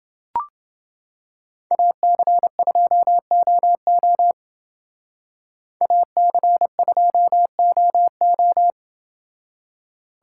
Morse code, heard as a keyed tone, about 27 wpm, 700 Hz. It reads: AC2OO AC2OO